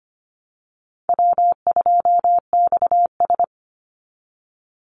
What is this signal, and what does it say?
Morse code, heard as a keyed tone, about 25 wpm, 700 Hz. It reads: W2XS
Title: Easy Call Signs Repeated 3x - 25wpm